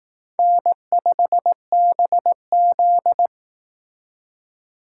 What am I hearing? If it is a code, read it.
Morse code, heard as a keyed tone, about 18 wpm, 700 Hz. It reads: N5BZ
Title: Speed Racing - USA 1x2 Call Signs 15wpm